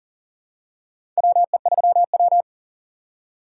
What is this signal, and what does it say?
Morse code, heard as a keyed tone, about 40 wpm, 700 Hz. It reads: WE3W
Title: USA Call Signs - 4 Characters - 4x Character Spacing - 40wpm